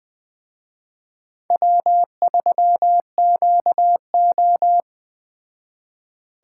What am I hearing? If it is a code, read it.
Morse code, heard as a keyed tone, about 20 wpm, 700 Hz. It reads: W3QO